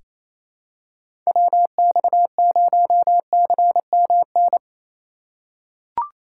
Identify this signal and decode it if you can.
Morse code, heard as a keyed tone, about 27 wpm, 700 Hz. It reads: WX0CMN
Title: USA Call Signs - 6 Characters - 2x Character Spacing - 28wpm